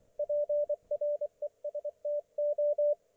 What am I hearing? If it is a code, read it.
Morse code, heard as a keyed tone, about 24 wpm, 590 Hz, with radio noise. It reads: PRESTO
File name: Archiv / CUC212 / ok1fhi cul212